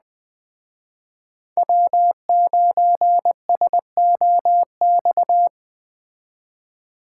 Morse code, heard as a keyed tone, about 20 wpm, 700 Hz. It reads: W9SOX
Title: Speed Racing - USA 1x3 Call Signs 17wpm